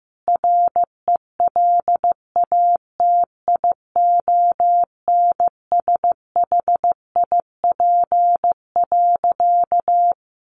Morse code, heard as a keyed tone, about 15 wpm, 700 Hz. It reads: RELATIONSHIP.
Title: Sentences from Top 1100 Words 15wpm